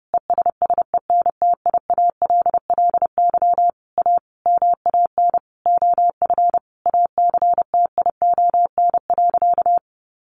Morse code, heard as a keyed tone, about 30 wpm, 700 Hz. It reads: ESSENTIALLY A MAN OF ACTION.
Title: Sentences from Top 1300 Words 30wpm